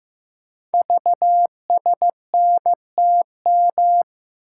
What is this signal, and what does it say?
Morse code, heard as a keyed tone, about 15 wpm, 700 Hz. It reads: VSNTM
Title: Code Groups - 5 Letter:Numbers - 15wpm